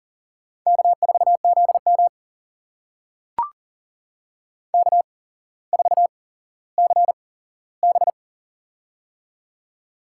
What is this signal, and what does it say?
Morse code, heard as a keyed tone, about 39 wpm, spaced as for 17 wpm, 700 Hz. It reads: K4ZM K4CB
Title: USA Call Signs - 4 Characters - 8x Character Spacing - 40wpm